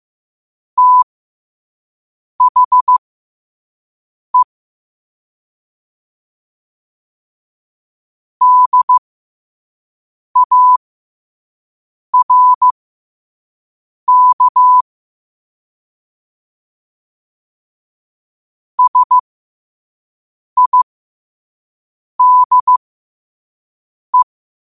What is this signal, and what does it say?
Morse code, heard as a keyed tone, about 15 wpm, spaced as for 5 wpm, 1000 Hz. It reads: THE DARK SIDE
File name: qotdfiles / 5wpm